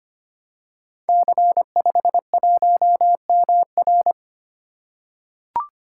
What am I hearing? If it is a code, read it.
Morse code, heard as a keyed tone, about 25 wpm, 700 Hz. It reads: C51MR
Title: Code Groups - 5 Letter:Numbers - 25wpm